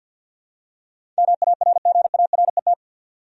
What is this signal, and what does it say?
Morse code, heard as a keyed tone, about 52 wpm, 700 Hz. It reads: MARGARET
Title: USA Common Names - 50wpm